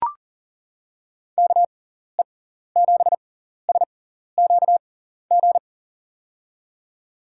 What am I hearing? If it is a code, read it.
Morse code, heard as a keyed tone, about 39 wpm, spaced as for 14 wpm, 700 Hz. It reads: KE7SQG
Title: USA Call Signs - 6 Characters - 6x Character Spacing - 40wpm